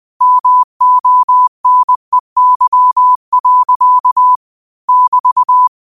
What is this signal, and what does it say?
Morse code, heard as a keyed tone, about 20 wpm, 1000 Hz. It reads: MONEY. <BT>
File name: qotdfiles / 20wpm